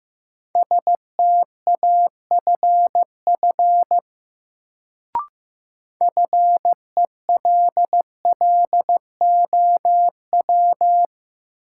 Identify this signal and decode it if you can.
Morse code, heard as a keyed tone, about 15 wpm, 700 Hz. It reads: STAFF FELLOW
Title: Sets of 5 Words Encore - New Word Drill - 15wpm